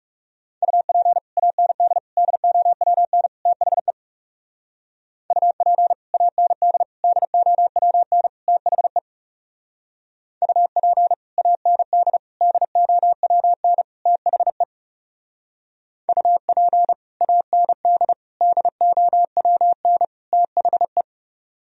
Morse code, heard as a keyed tone, about 36 wpm, 700 Hz. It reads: UP AND DOWN THE UP AND DOWN THE UP AND DOWN THE UP AND DOWN THE
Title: Speed-Racing - Sets of 4 Words 30wpm